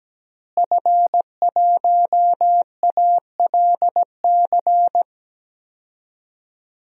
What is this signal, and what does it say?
Morse code, heard as a keyed tone, about 17 wpm, 700 Hz. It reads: F1ALC
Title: DXCC-100 Call Signs - 5 Characters - Repeated 3x - 17wpm